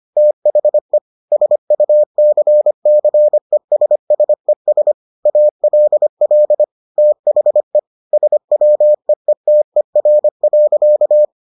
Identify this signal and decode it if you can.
Morse code, heard as a keyed tone, about 25 wpm, 600 Hz. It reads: THE SUCCESSES ALL THE SWEETER.